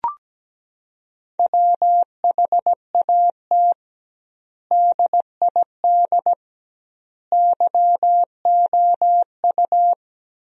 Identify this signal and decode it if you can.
Morse code, heard as a keyed tone, about 17 wpm, 700 Hz. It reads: WHAT DID YOU
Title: 2x Word-Spacing - Sentences from Top 200 Words - 17wpm